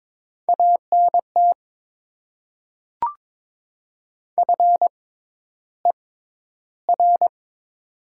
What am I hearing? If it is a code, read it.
Morse code, heard as a keyed tone, about 22 wpm, spaced as for 11 wpm, 700 Hz. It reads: ANT FER
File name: Top 20 QSO Elements - 6x Character Spacing - 22wpm